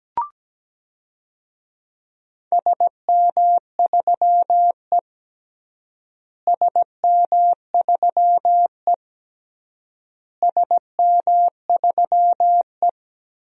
Morse code, heard as a keyed tone, about 17 wpm, 700 Hz. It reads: SM3E SM3E SM3E
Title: DXCC-100 Call Signs - 4 Characters - Repeated 3x - 17wpm